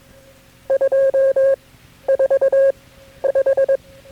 Morse code, heard as a keyed tone, about 22 wpm, spaced as for 12 wpm, 540 Hz. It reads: 245
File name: KB6MT-tape3-side1